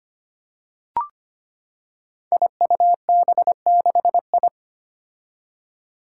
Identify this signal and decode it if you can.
Morse code, heard as a keyed tone, about 25 wpm, 700 Hz. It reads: IUB6I
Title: Five Challenging Letter-Number Combinations 25wpm